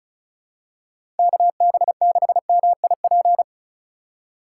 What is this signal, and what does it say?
Morse code, heard as a keyed tone, about 36 wpm, 700 Hz. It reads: KB6MIP
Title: USA Call Signs - 6 Characters - 3x Character Spacing - 35wpm